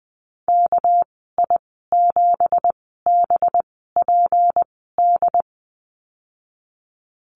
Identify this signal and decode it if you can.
Morse code, heard as a keyed tone, about 20 wpm, spaced as for 15 wpm, 700 Hz. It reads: KI7BPD
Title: 207 Lesson 33g - Letter j - Call Signs 20wpm